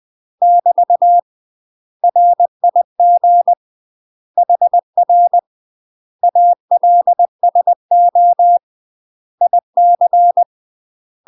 Morse code, heard as a keyed tone, about 20 wpm, 700 Hz. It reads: <BT> RIG HR ALSO IC
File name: UK MASTER QSOs TRAINING / 20WPM / UK MASTER QSOs 0045